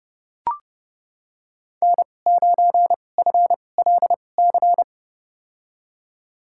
Morse code, heard as a keyed tone, about 30 wpm, spaced as for 22 wpm, 700 Hz. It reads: N9FLC
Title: USA Call Signs - 5 Characters - 2x Character Spacing - 30wpm